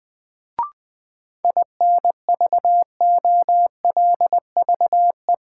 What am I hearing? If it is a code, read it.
Morse code, heard as a keyed tone, about 20 wpm, 700 Hz. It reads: INVOLVE
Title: Sentences from Top 1200 Words - New Word Drill Alt - 20wpm